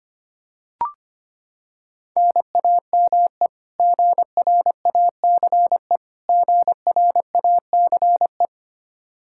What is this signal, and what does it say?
Morse code, heard as a keyed tone, about 25 wpm, 700 Hz. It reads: NAME GRACE GRACE
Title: Practice QSO - Exchange 1 - Beginning - 25wpm